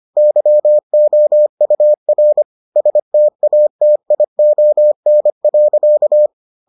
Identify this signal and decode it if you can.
Morse code, heard as a keyed tone, about 25 wpm, 600 Hz. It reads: YOUR STATION.